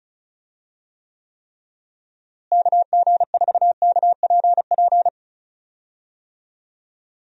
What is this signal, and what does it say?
Morse code, heard as a keyed tone, about 36 wpm, 700 Hz. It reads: KG4KPP
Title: USA Call Signs - 6 Characters - Repeated 3x - 35wpm